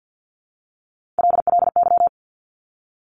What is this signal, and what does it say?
Morse code, heard as a keyed tone, about 50 wpm, 700 Hz. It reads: LLY
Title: Top Three Letter Combinations in Words 50wpm